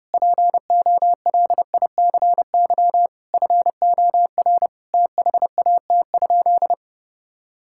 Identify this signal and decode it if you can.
Morse code, heard as a keyed tone, about 32 wpm, 700 Hz. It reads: POLICY FOR THAT?
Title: Sentences from Top 1000 Words - Review Part 5 - 30wpm